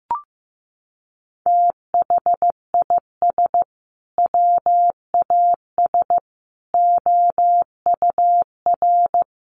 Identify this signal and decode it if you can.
Morse code, heard as a keyed tone, about 15 wpm, 700 Hz. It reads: THIS WAS OUR
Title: Sentences from Top 1200 Words 15wpm